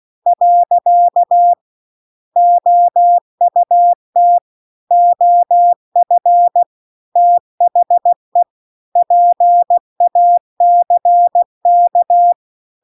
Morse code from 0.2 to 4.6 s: . OUT